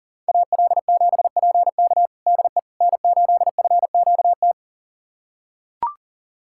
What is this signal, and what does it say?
Morse code, heard as a keyed tone, about 40 wpm, 700 Hz. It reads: AL7PK DE N8FQT
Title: USA Callsign DE Callsign - 5 Characters - 40wpm